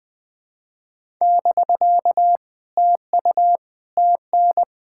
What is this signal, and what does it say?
Morse code, heard as a keyed tone, about 20 wpm, 700 Hz. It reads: <BK> TU TN
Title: POTA QSO Practice - 20wpm